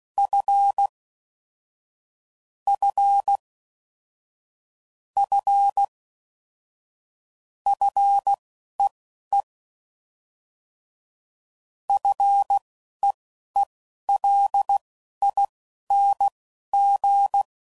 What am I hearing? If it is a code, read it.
Morse code, heard as a keyed tone, about 16 wpm, spaced as for 11 wpm, 800 Hz. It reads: F F F FEE FEELING
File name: Basic Sets / QST de WA7RSO / Combnations